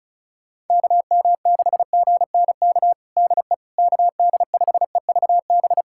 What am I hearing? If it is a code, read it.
Morse code, heard as a keyed tone, about 34 wpm, 700 Hz. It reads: KM6GNK DE KD5EVB
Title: USA Callsign DE Callsign - 6 Characters - 35wpm